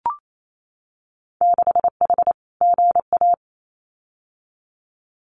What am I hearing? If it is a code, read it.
Morse code, heard as a keyed tone, about 28 wpm, 700 Hz. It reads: 6H GA
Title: Winter Field Day Exchanges Encore 28wpm